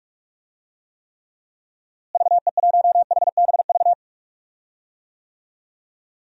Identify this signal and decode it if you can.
Morse code, heard as a keyed tone, about 46 wpm, 700 Hz. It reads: VE1HBV
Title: DXCC-100 Call Signs - 6 Characters - Repeated 3x - 45wpm